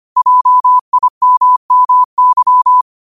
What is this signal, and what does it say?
Morse code, heard as a keyed tone, about 25 wpm, 1000 Hz. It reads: JIMMY
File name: qotdfiles / 25wpm